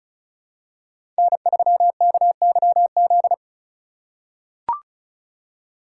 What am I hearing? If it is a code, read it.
Morse code, heard as a keyed tone, about 34 wpm, 700 Hz. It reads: N3KYZ
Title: USA Call Signs - 2x Character Spacing - 35wpm